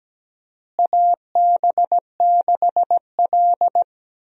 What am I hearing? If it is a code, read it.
Morse code, heard as a keyed tone, about 17 wpm, 700 Hz. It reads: AB6L
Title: USA Call Signs - 4 Characters - Repeated 3x - 17wpm